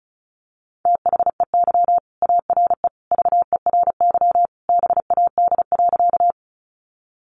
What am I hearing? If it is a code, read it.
Morse code, heard as a keyed tone, about 34 wpm, 700 Hz. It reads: THEY ARE VERY BAD.
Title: Sentences from Top 700 Words 35wpm